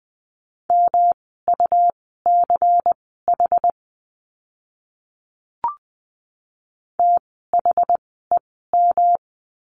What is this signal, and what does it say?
Morse code, heard as a keyed tone, about 20 wpm, spaced as for 14 wpm, 700 Hz. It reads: MUCH THEM
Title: 108 Lesson 18f - Letter m - Words 20wpm